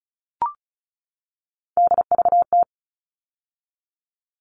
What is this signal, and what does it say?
Morse code, heard as a keyed tone, about 34 wpm, 700 Hz. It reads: DVT
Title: All Cut Numbers - 3-Digit 35wpm